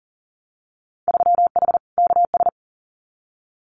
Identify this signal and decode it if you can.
Morse code, heard as a keyed tone, about 38 wpm, 700 Hz. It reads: 3H KS
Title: Winter Field Day Exchanges Encore 40wpm